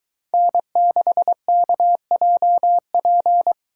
Morse code, heard as a keyed tone, about 22 wpm, 700 Hz. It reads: N6KJP